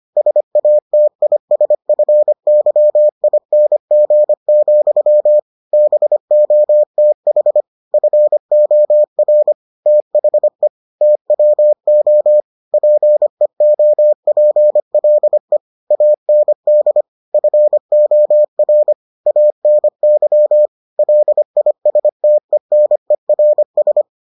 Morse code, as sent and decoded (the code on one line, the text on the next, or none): ... .- - .. ... ..-. -.-- .. -. --. --..-- / -... --- - .... / ..-. --- .-. / - .... / - .-- --- / .--. --- .--. .-.. / .- -. -.. / ..-. --- .-. / .- -. -.-- / .-.. .. ... - . -. .-. ...
SATISFYING, BOTH FOR THE TWO PEOPLE AND FOR ANY LISTENERS